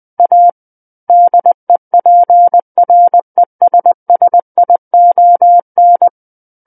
Morse code, heard as a keyed tone, about 20 wpm, 700 Hz. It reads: A DEPRESSION
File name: 20 WPM